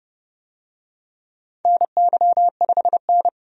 Morse code, heard as a keyed tone, about 30 wpm, 700 Hz. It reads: NY5N
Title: USA Call Signs - 4 Characters - 30wpm